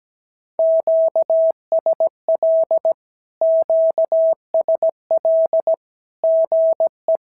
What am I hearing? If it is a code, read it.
Morse code, heard as a keyed tone, about 17 wpm, 650 Hz. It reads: QSL QSL GE
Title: SOTA QSO Practice 17wpm